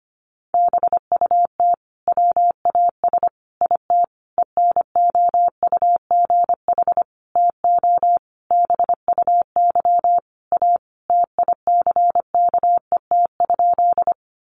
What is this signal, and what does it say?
Morse code, heard as a keyed tone, about 25 wpm, 700 Hz. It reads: BUT WAS IT ENOUGH TO BUY A TICKET?
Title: Sentences from Top 1400 Words 25wpm